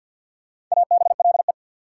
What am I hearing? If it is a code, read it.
Morse code, heard as a keyed tone, about 54 wpm, 690 Hz. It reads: ABLE
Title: Prefixes and Suffixes 50wpm